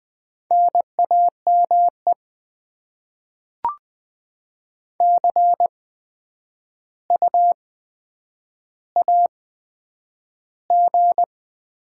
Morse code, heard as a keyed tone, about 20 wpm, spaced as for 10 wpm, 700 Hz. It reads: NAME CU AG
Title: Top 20 QSO Elements - 8x Character Spacing - 20wpm